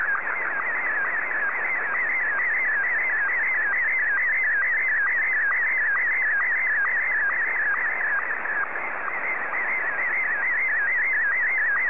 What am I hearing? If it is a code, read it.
SSTV (slow-scan television), Martin 1.